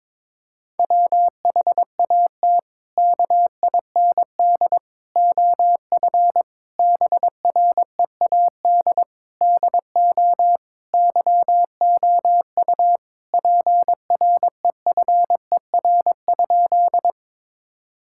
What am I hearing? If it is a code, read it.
Morse code, heard as a keyed tone, about 22 wpm, 700 Hz. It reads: WHAT KIND OF BREAD DO YOU PREFER?
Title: Sentences from Top 1600 Words 22wpm